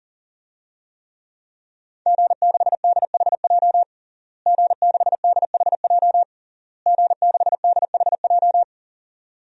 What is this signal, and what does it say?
Morse code, heard as a keyed tone, about 40 wpm, 700 Hz. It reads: G6DHJ G6DHJ G6DHJ
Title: DXCC-100 Call Signs - 5 Characters - Repeated 3x - 40wpm